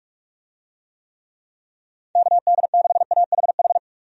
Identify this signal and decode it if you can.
Morse code, heard as a keyed tone, about 45 wpm, 700 Hz. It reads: KD6AHH
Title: USA Call Signs - 6 Characters - 45wpm